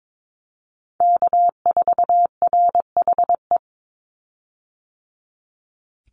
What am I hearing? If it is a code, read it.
Morse code, heard as a keyed tone, about 22 wpm, 700 Hz. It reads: K4RHE
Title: USA Call Signs 22wpm